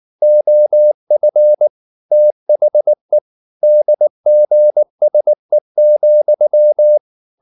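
Morse code, heard as a keyed tone, about 19 wpm, 600 Hz. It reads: OF THE DGSE,
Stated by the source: Morse code at 19 wpm from CW med Gnister, reading Ken Follett's novel Never in English.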